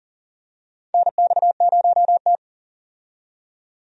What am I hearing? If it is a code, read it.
Morse code, heard as a keyed tone, about 40 wpm, 700 Hz. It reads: NX0T
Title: USA Call Signs - 4 Characters - 40wpm